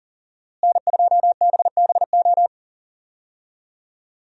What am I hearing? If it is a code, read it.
Morse code, heard as a keyed tone, about 40 wpm, 700 Hz. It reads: N2BBO
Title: USA Call Signs - 5 Characters - Repeated 3x - 40wpm